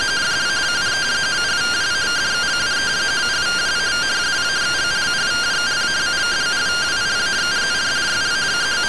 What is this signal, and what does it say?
Two-tone teleprinter signal, about 170 Hz shift.